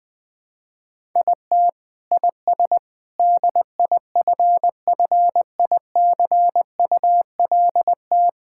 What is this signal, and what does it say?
Morse code, heard as a keyed tone, about 21 wpm, 700 Hz. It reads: IT IS DIFFICULT
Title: Sentences with 4-Word Phrases - 20wpm